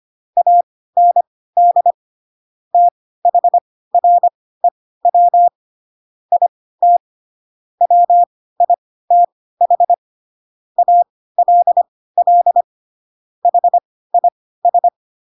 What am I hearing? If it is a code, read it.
Morse code, heard as a keyed tone, about 25 wpm, spaced as for 16 wpm, 700 Hz. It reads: AND THREW IT WITH ALL HIS
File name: Pinocchio / Pinocchio ch4 10 25 ik1hge